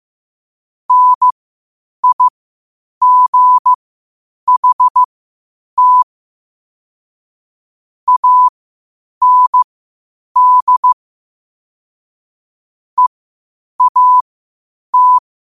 Morse code, heard as a keyed tone, about 15 wpm, spaced as for 8 wpm, 1000 Hz. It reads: NIGHT AND EAT
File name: qotdfiles / 8wpm